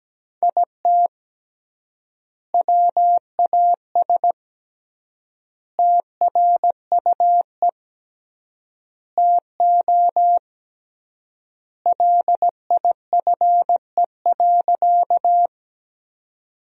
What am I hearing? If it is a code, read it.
Morse code, heard as a keyed tone, about 17 wpm, 700 Hz. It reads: IT WAS TRUE TO LIFE.
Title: 3x Word-Spacing - Sentences from Top 400 Words - 17wpm